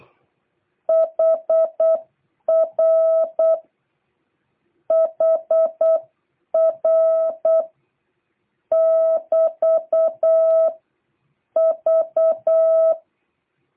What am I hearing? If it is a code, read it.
Morse code, heard as a keyed tone, about 8 wpm, 650 Hz. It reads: HR HR <BT> V